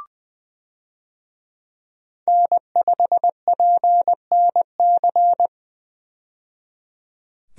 Morse code, heard as a keyed tone, about 20 wpm, 700 Hz. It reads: N5PNC
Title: USA Call Signs - 5 Characters - 20wpm